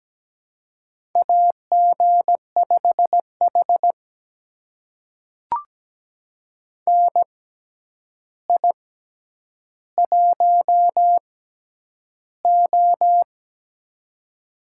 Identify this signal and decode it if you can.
Morse code, heard as a keyed tone, about 17 wpm, spaced as for 9 wpm, 700 Hz. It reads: AG5H NI1O
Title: USA Call Signs - 4 Characters - 6x Character Spacing - 17wpm